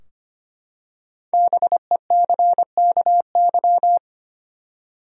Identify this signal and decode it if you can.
Morse code, heard as a keyed tone, about 25 wpm, 700 Hz. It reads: BECKY
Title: USA Common Names - 25wpm